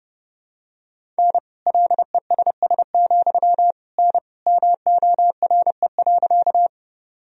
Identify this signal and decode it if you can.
Morse code, heard as a keyed tone, about 32 wpm, 700 Hz. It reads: N LESS, N MORE.
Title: Sentences from Top 400 Words and Abbreviations 30wpm